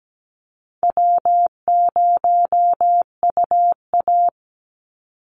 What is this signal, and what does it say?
Morse code, heard as a keyed tone, about 17 wpm, 700 Hz. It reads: W0UA
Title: Call Signs - 17wpm